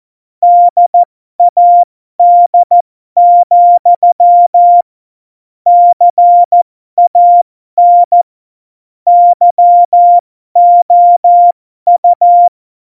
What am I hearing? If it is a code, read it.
Morse code, heard as a keyed tone, about 14 wpm, spaced as for 12 wpm, 700 Hz. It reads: DAD, CAN YOU
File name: Morsekurs / Witze